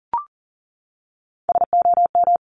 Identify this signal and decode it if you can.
Morse code, heard as a keyed tone, about 38 wpm, 690 Hz. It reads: SOM
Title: Top Three Letter Combinations in Words 40wpm